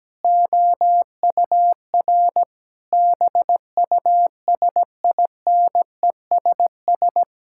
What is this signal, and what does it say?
Morse code, heard as a keyed tone, about 17 wpm, 700 Hz. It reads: OUR BUSINESS
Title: Sentences from Top 1000 Words - Review Part 5 - 17wpm